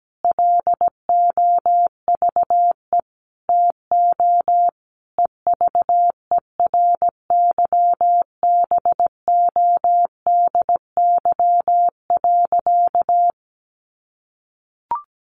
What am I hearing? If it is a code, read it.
Morse code, heard as a keyed tone, about 17 wpm, 700 Hz. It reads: LOVE TO EVERYBODY.
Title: Sentences from Top 1400 Words 17wpm